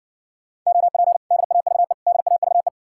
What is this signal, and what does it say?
Morse code, heard as a keyed tone, about 60 wpm, 700 Hz. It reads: OP DAVE DAVE